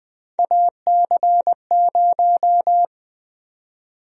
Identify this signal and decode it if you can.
Morse code, heard as a keyed tone, about 20 wpm, 700 Hz. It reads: AC0
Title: USA Call Sign Prefixes - Repeated 2x - 20wpm